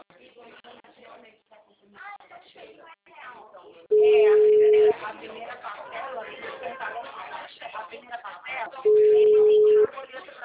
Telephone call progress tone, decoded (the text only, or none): ringback tone